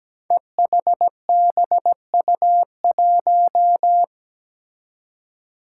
Morse code, heard as a keyed tone, about 17 wpm, 700 Hz. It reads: EHBU1
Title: Five Challenging Letter-Number Combinations 17wpm